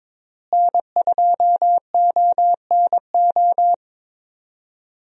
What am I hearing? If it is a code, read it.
Morse code, heard as a keyed tone, about 22 wpm, 700 Hz. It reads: N2ONO
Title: USA Call Signs - 5 Characters - Repeated 3x - 22wpm